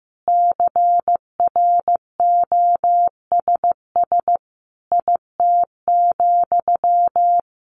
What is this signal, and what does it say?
Morse code, heard as a keyed tone, about 14 wpm, 700 Hz. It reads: CROSS IT,
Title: Sentences from Top 1400 Words 15wpm